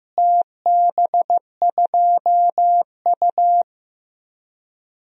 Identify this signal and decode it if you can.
Morse code, heard as a keyed tone, about 15 wpm, 700 Hz. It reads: TB2U
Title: DXCC-100 Call Signs - 4 Characters - Repeated 3x - 15wpm